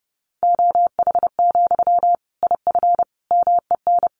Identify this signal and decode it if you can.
Morse code, heard as a keyed tone, about 30 wpm, 700 Hz. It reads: OH, IF MEN
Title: Sentences from Top 500 Words 30wpm